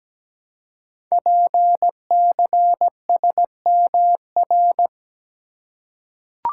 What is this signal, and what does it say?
Morse code, heard as a keyed tone, about 17 wpm, 700 Hz. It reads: PCSMR
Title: Code Groups - 5 Letter:Numbers - 17wpm